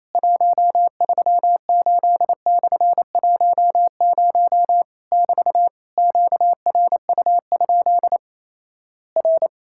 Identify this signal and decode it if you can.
Morse code, heard as a keyed tone, about 27 wpm, 700 Hz. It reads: 138/10 <BT> QRU? R
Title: Worked All Europe DX Contest - QTC Exchange - 28wpm